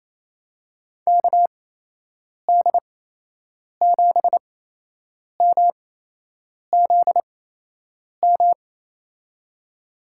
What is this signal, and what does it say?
Morse code, heard as a keyed tone, about 28 wpm, spaced as for 8 wpm, 700 Hz. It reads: KD7MZM
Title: USA Call Signs - 8x Character Spacing - 28wpm